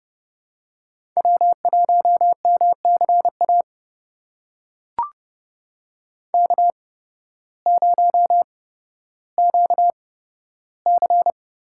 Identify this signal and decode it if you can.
Morse code, heard as a keyed tone, about 30 wpm, spaced as for 14 wpm, 700 Hz. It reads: W1MCA K0QC
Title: USA Call Signs - 5 Characters - 8x Character Spacing - 30wpm